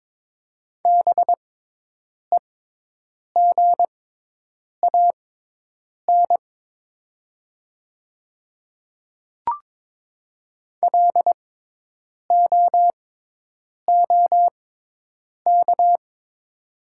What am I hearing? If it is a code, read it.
Morse code, heard as a keyed tone, about 22 wpm, spaced as for 8 wpm, 700 Hz. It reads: BEGAN LOOK
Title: 6x Character Spacing - Top 300 Words - 22wpm